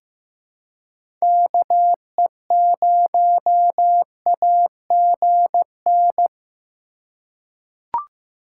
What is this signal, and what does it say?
Morse code, heard as a keyed tone, about 15 wpm, 700 Hz. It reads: KE0AGN
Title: USA Call Signs - 2x Character Spacing - 15wpm